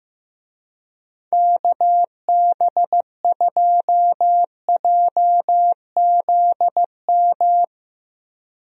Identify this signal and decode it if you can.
Morse code, heard as a keyed tone, about 15 wpm, 700 Hz. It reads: KB2JZM
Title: USA Call Signs - 6 Characters - Repeated 3x - 15wpm